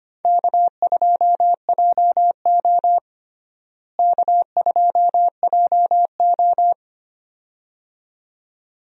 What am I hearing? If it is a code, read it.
Morse code, heard as a keyed tone, about 25 wpm, 700 Hz. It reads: K2JO K2JO
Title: USA Call Signs - 4 Characters - Repeated 2x - 25wpm